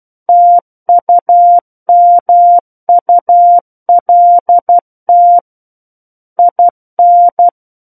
Morse code, heard as a keyed tone, about 12 wpm, 700 Hz. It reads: TUMULT IN
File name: Book One / 03-Chapter 3 - 12 WPM